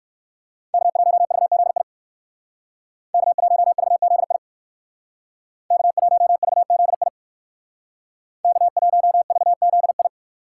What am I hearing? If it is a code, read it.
Morse code, heard as a keyed tone, about 56 wpm, 700 Hz. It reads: K1VZI K1VZI K1VZI K1VZI